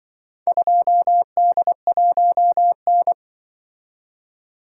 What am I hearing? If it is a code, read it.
Morse code, heard as a keyed tone, about 25 wpm, 700 Hz. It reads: 2D1N
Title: Speed-Racing - Four Letter-Number Combinations - 20wpm